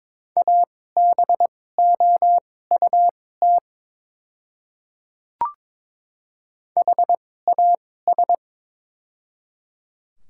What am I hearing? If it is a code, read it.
Morse code, heard as a keyed tone, about 22 wpm, spaced as for 16 wpm, 700 Hz. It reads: ABOUT HAS
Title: 2x Character Spacing - Top 100 Words - 22wpm